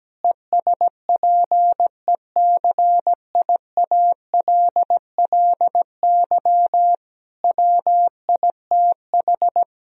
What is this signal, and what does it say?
Morse code, heard as a keyed tone, about 17 wpm, 700 Hz. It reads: ESPECIALLY WITH
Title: Sentences from Top 1000 Words - Review Part 4 - 17wpm